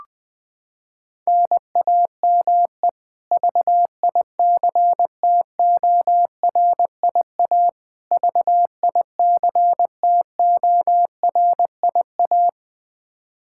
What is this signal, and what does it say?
Morse code, heard as a keyed tone, about 21 wpm, 700 Hz. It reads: NAME VICTORIA VICTORIA
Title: K1USN SST QSO Practice 20wpm